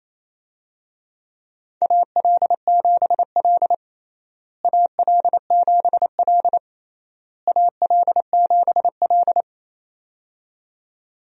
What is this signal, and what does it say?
Morse code, heard as a keyed tone, about 28 wpm, 700 Hz. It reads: AL7L AL7L AL7L
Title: USA Call Signs - 4 Characters - Repeated 3x - 28wpm